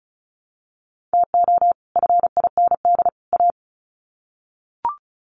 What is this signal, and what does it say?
Morse code, heard as a keyed tone, about 34 wpm, 700 Hz. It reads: TO FIND A
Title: Sets of 3 Words 35wpm